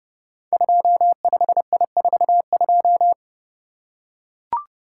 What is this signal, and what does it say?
Morse code, heard as a keyed tone, about 28 wpm, 700 Hz. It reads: 25I42
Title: Code Groups - 5 Letter:Numbers - 30wpm